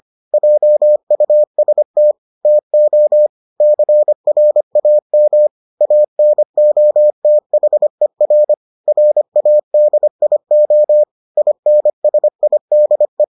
Morse code, heard as a keyed tone, about 25 wpm, 600 Hz. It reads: JUST TO CRAM ANOTHER RADIO INSIDE